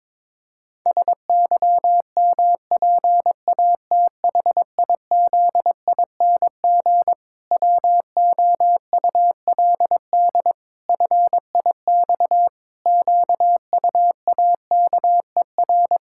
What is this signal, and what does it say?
Morse code, heard as a keyed tone, about 22 wpm, 700 Hz. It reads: SYMPATHIZING WOULD FIX QUAKER
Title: Pangram Challenge v2 - 22wpm